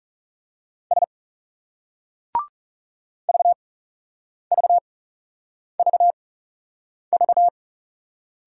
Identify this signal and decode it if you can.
Morse code, heard as a keyed tone, about 36 wpm, spaced as for 8 wpm, 700 Hz. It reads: S VVVV
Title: Speed-Racing Single Letters 30wpm